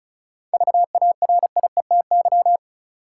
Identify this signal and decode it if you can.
Morse code, heard as a keyed tone, about 34 wpm, 700 Hz. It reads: VARIETY
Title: Seven Letter Words - 35wpm